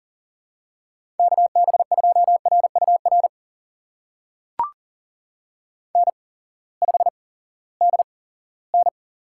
Morse code, heard as a keyed tone, about 39 wpm, spaced as for 20 wpm, 700 Hz. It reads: KB2RUR N5DN
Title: USA Call Signs - 8x Character Spacing - 40wpm